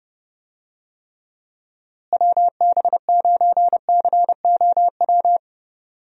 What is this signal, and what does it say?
Morse code, heard as a keyed tone, about 30 wpm, 700 Hz. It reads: WB9COW
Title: USA Call Signs - 6 Characters - Repeated 2x - 30wpm